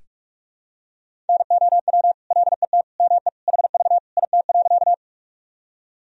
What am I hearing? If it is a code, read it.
Morse code, heard as a keyed tone, about 46 wpm, 700 Hz. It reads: NOW LET ME HV IT.
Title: Sentences from Top 300 Words and Abbreviations 45wpm